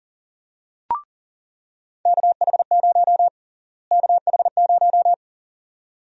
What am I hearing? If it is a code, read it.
Morse code, heard as a keyed tone, about 40 wpm, 700 Hz. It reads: KH0 KH0
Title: USA Call Sign Prefixes - Repeated 2x - 40wpm